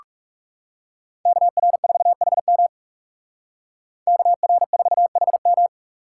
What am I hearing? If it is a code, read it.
Morse code, heard as a keyed tone, about 36 wpm, 700 Hz. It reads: KR4HM KR4HM
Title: Speed Racing - USA 2x2 Call Signs 30wpm